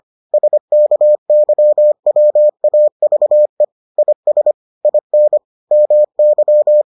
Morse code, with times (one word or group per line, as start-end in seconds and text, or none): 0.3-3.6 SKYWAVE
4.0-4.5 IS
4.8-5.4 IN
5.7-6.8 MY